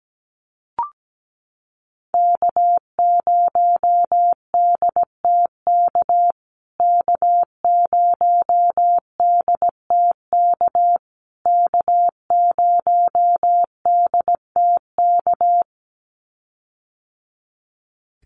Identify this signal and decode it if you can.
Morse code, heard as a keyed tone, about 17 wpm, 700 Hz. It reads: K0DTK K0DTK K0DTK
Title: USA Call Signs Repeated 3x 17wpm